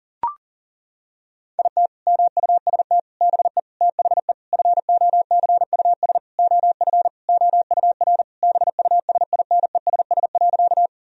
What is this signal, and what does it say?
Morse code, heard as a keyed tone, about 40 wpm, 700 Hz. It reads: IT MUST BE THE FOCUS OF OUR BUSINESS.
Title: Sentences from Top 1000 Words - Review Part 4 - 40wpm